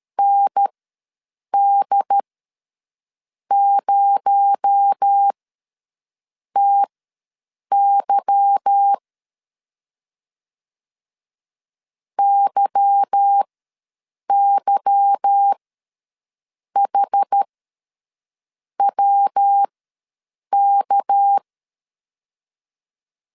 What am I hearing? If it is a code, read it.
Morse code, heard as a keyed tone, about 13 wpm, spaced as for 6 wpm, 800 Hz. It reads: ND0TY YYHWK
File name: LEZIONI / 8 YJ45 / 8 Y ws 12wpm 3409